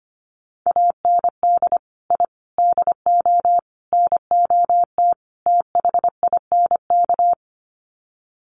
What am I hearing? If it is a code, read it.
Morse code, heard as a keyed tone, about 25 wpm, 700 Hz. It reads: AND I DO NOT THINK
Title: Sets of 5 Words 25wpm